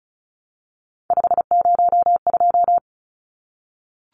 Morse code, heard as a keyed tone, about 34 wpm, 700 Hz. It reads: 502
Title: Numbers - 3-Digit 35wpm